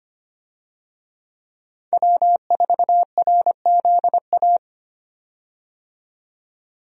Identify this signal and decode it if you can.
Morse code, heard as a keyed tone, about 25 wpm, 700 Hz. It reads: W4RZA
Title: USA Call Signs - 5 Characters - 25wpm